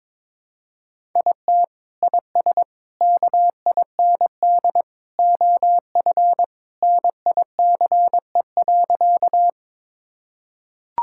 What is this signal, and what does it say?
Morse code, heard as a keyed tone, about 22 wpm, 700 Hz. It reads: IT IS KIND OF NICE.
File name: practice / 22wpm / Sentences with 2-Word Phrases - 22wpm